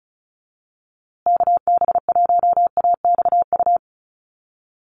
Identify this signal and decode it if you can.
Morse code, heard as a keyed tone, about 34 wpm, 700 Hz. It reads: KB1AXU
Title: USA Call Signs Repeated 3x 35wpm